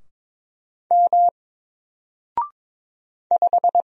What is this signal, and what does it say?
Morse code, heard as a keyed tone, about 22 wpm, 700 Hz. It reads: M 5
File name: Single-Letter-Number-22wpm